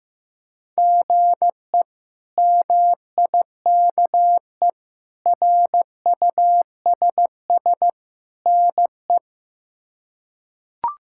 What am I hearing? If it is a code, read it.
Morse code, heard as a keyed tone, about 15 wpm, 700 Hz. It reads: GE MIKE RUSS NE
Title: K1USN SST QSO Practice - 15wpm